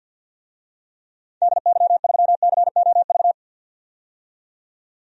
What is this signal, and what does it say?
Morse code, heard as a keyed tone, about 52 wpm, 700 Hz. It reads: DY3COV
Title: DXCC-100 Call Signs - 6 Characters - 50wpm